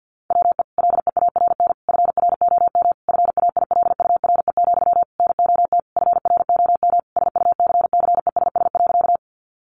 Morse code, heard as a keyed tone, about 50 wpm, 700 Hz. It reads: WE LEARN FROM FAILURE, NOT FROM SUCCESS.
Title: Sentences from Top 900 Words 50wpm